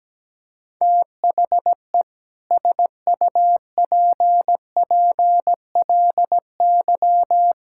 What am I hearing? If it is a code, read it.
Morse code, heard as a keyed tone, about 17 wpm, 700 Hz. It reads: THE SUPPLY
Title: Sentences from Top 1000 Words - Review Part 6 - 17wpm